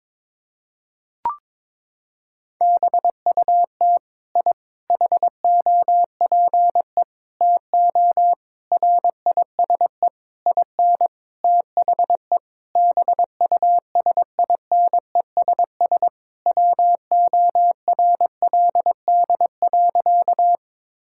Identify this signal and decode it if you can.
Morse code, heard as a keyed tone, about 22 wpm, 700 Hz. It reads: BUT I HOPE TO RISE IN THE BUSINESS WORLD.
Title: Sentences from Top 1000 Words - Review Part 6 - 22wpm